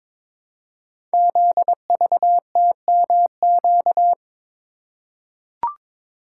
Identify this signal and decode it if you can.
Morse code, heard as a keyed tone, about 22 wpm, 700 Hz. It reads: ZVTMQ
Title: Code Groups - 5 Letters - 22wpm